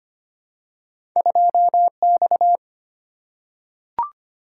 Morse code, heard as a keyed tone, about 25 wpm, 690 Hz. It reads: 2X
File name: Two Letter-Number Combinations - 25wpm